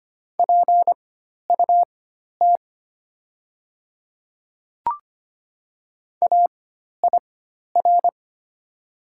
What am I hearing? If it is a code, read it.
Morse code, heard as a keyed tone, about 25 wpm, spaced as for 12 wpm, 690 Hz. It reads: PUT AIR
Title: 4x Character Spacing - Top 200 Words - 25wpm